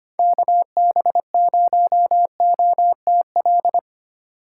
Morse code, heard as a keyed tone, about 25 wpm, 700 Hz. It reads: KB0OTL
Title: USA Call Signs - 6 Characters - 3x Character Spacing - 25wpm